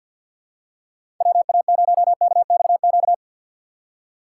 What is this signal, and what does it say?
Morse code, heard as a keyed tone, about 48 wpm, 700 Hz. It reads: WA0KXQ
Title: USA Call Signs - 4x Character Spacing - 50wpm